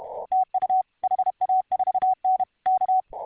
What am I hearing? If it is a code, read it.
Morse code, heard as a keyed tone, about 33 wpm, 750 Hz. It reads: TU HA4N K